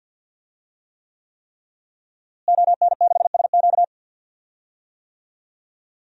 Morse code, heard as a keyed tone, about 50 wpm, 700 Hz. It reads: ON6SQ
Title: DXCC-100 Call Signs - 5 Characters - Repeated 3x - 50wpm